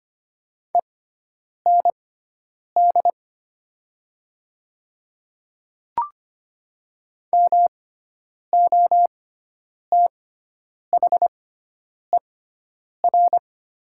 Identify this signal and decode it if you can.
Morse code, heard as a keyed tone, about 25 wpm, spaced as for 9 wpm, 700 Hz. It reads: END MOTHER
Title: 6x Character Spacing - Top 200 Words - 25wpm